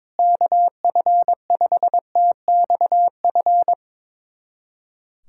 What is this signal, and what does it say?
Morse code, heard as a keyed tone, about 22 wpm, 700 Hz. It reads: KF5TXF
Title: USA Callsign DE Callsign 22wpm